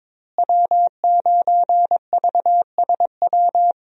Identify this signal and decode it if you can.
Morse code, heard as a keyed tone, about 22 wpm, 700 Hz. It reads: W9VSW
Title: USA Call Signs - 5 Characters - Repeated 3x - 22wpm